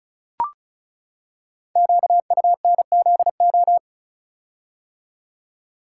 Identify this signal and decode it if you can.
Morse code, heard as a keyed tone, about 36 wpm, 690 Hz. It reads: QUNZO
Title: Code Groups - 5 Letters - 35wpm